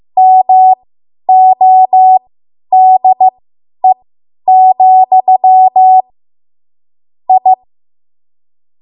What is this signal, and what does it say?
Morse code, heard as a keyed tone, about 15 wpm, spaced as for 10 wpm, 750 Hz. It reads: MODE, I
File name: Morse / 10WPM